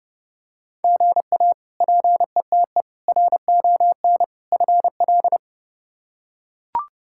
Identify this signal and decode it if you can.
Morse code, heard as a keyed tone, about 30 wpm, 700 Hz. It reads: GA PETE RON FL
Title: K1USN SST QSO Practice - 30wpm